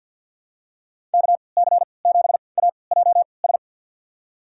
Morse code, heard as a keyed tone, about 53 wpm, spaced as for 29 wpm, 700 Hz. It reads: KC7AJS
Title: USA Call Signs - 3x Character Spacing - 50wpm